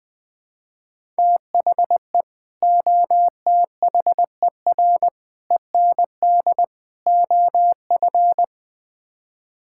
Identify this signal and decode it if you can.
Morse code, heard as a keyed tone, about 20 wpm, 700 Hz. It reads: THE OTHER END OF
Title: Build A Phrase - 5-Word Phrases - 20wpm